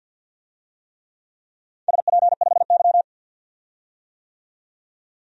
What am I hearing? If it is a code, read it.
Morse code, heard as a keyed tone, about 52 wpm, 700 Hz. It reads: SP5Y
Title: DXCC-100 Call Signs - 4 Characters - 50wpm